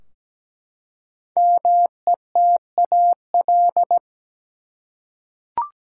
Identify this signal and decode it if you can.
Morse code, heard as a keyed tone, about 17 wpm, 700 Hz. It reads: METAL
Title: Five Letter Words - 17wpm